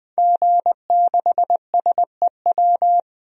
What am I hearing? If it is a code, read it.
Morse code, heard as a keyed tone, about 20 wpm, 700 Hz. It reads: G6SEW
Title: DXCC-100 Call Signs - 5 Characters - Repeated 2x - 20wpm